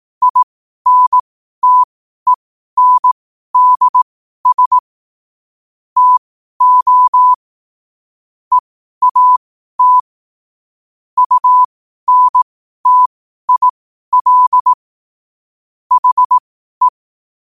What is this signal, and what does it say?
Morse code, heard as a keyed tone, about 18 wpm, spaced as for 12 wpm, 1000 Hz. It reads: INTENDS TO EAT UNTIL HE
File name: qotdfiles / 12wpm